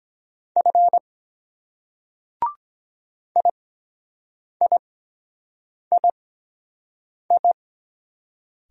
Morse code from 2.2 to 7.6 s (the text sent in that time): IIII